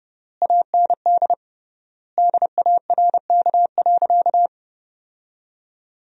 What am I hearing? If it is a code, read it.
Morse code, heard as a keyed tone, about 29 wpm, 700 Hz. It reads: AND DARK.
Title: 3x Word-Spacing - Sentences from Top 500 Words - 30wpm